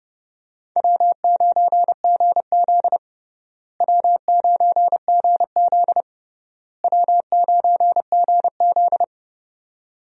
Morse code, heard as a keyed tone, about 30 wpm, 700 Hz. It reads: W9GZ W9GZ W9GZ
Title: USA Call Signs - 4 Characters - Repeated 3x - 30wpm